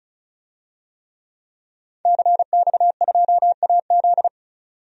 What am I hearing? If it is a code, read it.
Morse code, heard as a keyed tone, about 34 wpm, 700 Hz. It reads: CX2AZ
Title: DXCC-100 Call Signs - 5 Characters - 35wpm